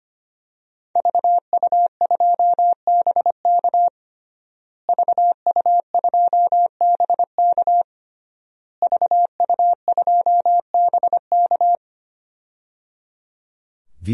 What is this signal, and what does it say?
Morse code, heard as a keyed tone, about 25 wpm, 700 Hz. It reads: VU2BK VU2BK VU2BK
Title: DXCC-100 Call Signs - 5 Characters - Repeated 3x - 25wpm